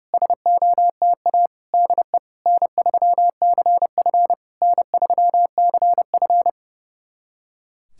Morse code, heard as a keyed tone, about 30 wpm, 700 Hz. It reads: SOTA DE N3CF N3CF
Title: SOTA QSO Practice 30wpm